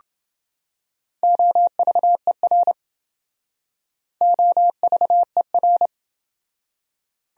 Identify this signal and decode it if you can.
Morse code, heard as a keyed tone, about 28 wpm, 700 Hz. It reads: OVER OVER